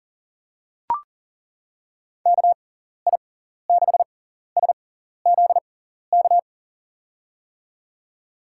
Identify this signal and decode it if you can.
Morse code, heard as a keyed tone, about 40 wpm, spaced as for 14 wpm, 700 Hz. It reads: KI6SZK
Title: USA Call Signs - 6x Character Spacing - 40wpm